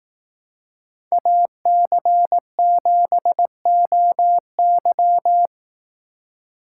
Morse code, heard as a keyed tone, about 18 wpm, 700 Hz. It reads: AC7OY
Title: Speed Racing - USA 2x2 Call Signs 15wpm